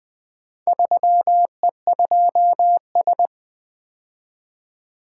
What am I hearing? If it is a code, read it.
Morse code, heard as a keyed tone, about 24 wpm, 700 Hz. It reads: 3E2S
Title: Speed-Racing - Four Letter-Number Combinations - 20wpm